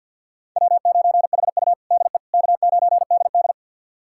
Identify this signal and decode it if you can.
Morse code, heard as a keyed tone, about 50 wpm, 700 Hz. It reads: W9HU DE K9DD
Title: USA Callsign DE Callsign - 4 Characters - 50wpm